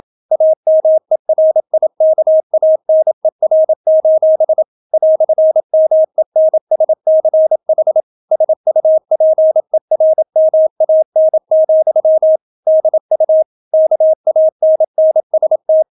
Morse code, heard as a keyed tone, about 27 wpm, 630 Hz. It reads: AMERIKANER: "MENSCH SUPERMAN, DU KANNST